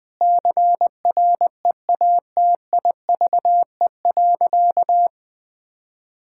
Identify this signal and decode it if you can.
Morse code, heard as a keyed tone, about 21 wpm, 700 Hz. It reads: CREATIVE.
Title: Sentences from Top 1000 Words - Review Part 5 - 20wpm